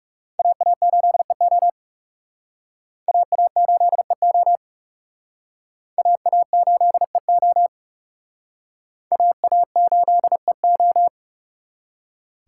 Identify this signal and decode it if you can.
Morse code, heard as a keyed tone, about 37 wpm, 700 Hz. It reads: AA8EO AA8EO AA8EO AA8EO
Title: Speed Racing - USA 2x2 Call Signs 30wpm